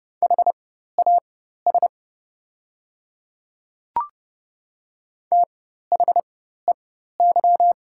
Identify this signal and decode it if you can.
Morse code, heard as a keyed tone, about 30 wpm, spaced as for 14 wpm, 700 Hz. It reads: HAS THEY
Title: 4x Character Spacing - Top 100 Words - 30wpm